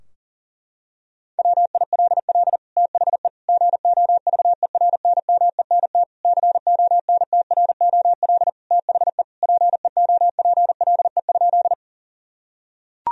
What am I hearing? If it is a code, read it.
Morse code, heard as a keyed tone, about 40 wpm, 700 Hz. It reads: WILL THE GOVERNMENT CONTROL THE PEOPLE?
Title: Sentences from Top 1000 Words - Review Part 3 - 40wpm